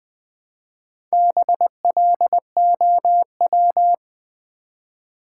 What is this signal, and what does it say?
Morse code, heard as a keyed tone, about 20 wpm, 700 Hz. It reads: BLOW
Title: Build A Word - 1 Letter At A Time - 4-Letter Words - 20wpm